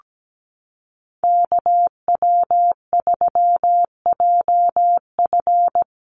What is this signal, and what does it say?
Morse code, heard as a keyed tone, about 17 wpm, 700 Hz. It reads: KW3JF
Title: USA Call Signs Repeated 3x 17wpm